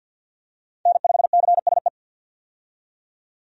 Morse code, heard as a keyed tone, about 51 wpm, 700 Hz. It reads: N5CSE
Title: USA Call Signs - 5 Characters - 50wpm